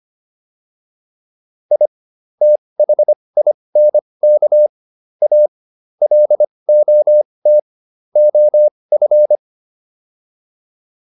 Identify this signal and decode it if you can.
Morse code, heard as a keyed tone, about 25 wpm, spaced as for 20 wpm, 600 Hz. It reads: I THINK A LOT OF